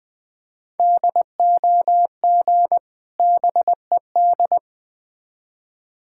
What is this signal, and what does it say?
Morse code, heard as a keyed tone, about 20 wpm, 700 Hz. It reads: DOG BED
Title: Difficult Words with Hints - 20wpm